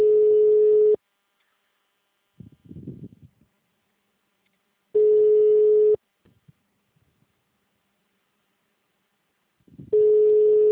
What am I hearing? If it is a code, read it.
Ringback tone (Europe, 425 Hz).